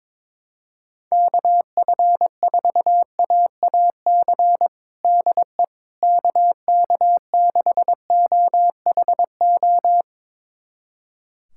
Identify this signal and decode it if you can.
Morse code, heard as a keyed tone, about 22 wpm, 700 Hz. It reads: KF4AAC DE KK6OHO
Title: USA Callsign DE Callsign 22wpm